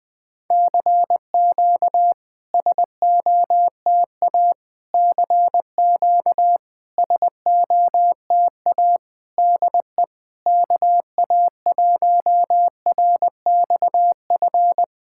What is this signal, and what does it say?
Morse code, heard as a keyed tone, about 20 wpm, 690 Hz. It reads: CQ SOTA CQ SOTA DE KA1RXF
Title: SOTA QSO Practice 20wpm